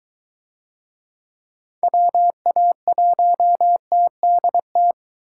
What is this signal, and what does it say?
Morse code, heard as a keyed tone, about 23 wpm, 700 Hz. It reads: WA1TDT